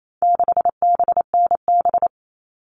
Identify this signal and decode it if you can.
Morse code, heard as a keyed tone, about 28 wpm, 700 Hz. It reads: 6BNB
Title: All Cut Numbers - 4-Digit 28wpm